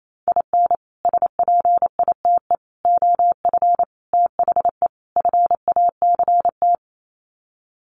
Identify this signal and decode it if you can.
Morse code, heard as a keyed tone, about 28 wpm, 700 Hz. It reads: IN SPITE OF THE FACT
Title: Sets of 5 Words Encore2 28wpm